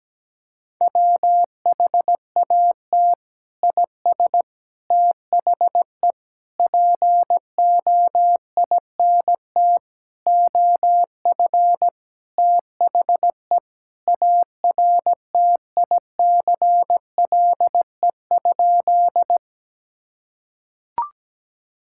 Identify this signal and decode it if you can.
Morse code, heard as a keyed tone, about 17 wpm, 700 Hz. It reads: WHAT IS THE POINT OF THE ARTICLE?
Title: Sentences from Top 1000 Words - Review Part 5 - 17wpm